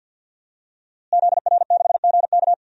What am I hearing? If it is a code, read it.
Morse code, heard as a keyed tone, about 48 wpm, 700 Hz. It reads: ZR6GK
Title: DXCC-100 Call Signs - 5 Characters - Repeated 3x - 50wpm